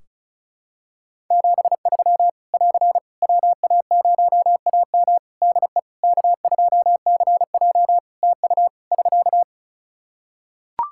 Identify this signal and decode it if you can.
Morse code, heard as a keyed tone, about 34 wpm, 700 Hz. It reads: 73 <AR> WA0AM DE K2CJ TU <SK>
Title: K1USN SST QSO Practice 35wpm